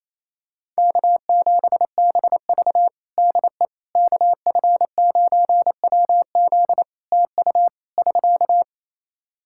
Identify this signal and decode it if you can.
Morse code, heard as a keyed tone, about 28 wpm, 700 Hz. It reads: K7BV DE KF9WZ TU <SK>
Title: K1USN SST QSO Practice 28wpm